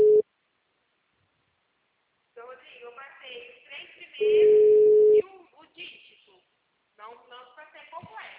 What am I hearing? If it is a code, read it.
Ringback tone (Europe, 425 Hz), with speech.